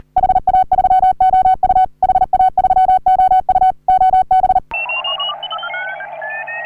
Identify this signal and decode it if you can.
Morse code, heard as a keyed tone, about 40 wpm, 750 Hz. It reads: HA3OU HA3OU OB
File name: CW / HA3OU RHOBCW REC